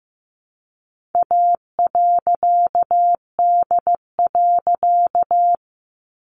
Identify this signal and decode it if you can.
Morse code, heard as a keyed tone, about 15 wpm, 700 Hz. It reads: A.D.
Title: A Sound Of Thunder 15wpm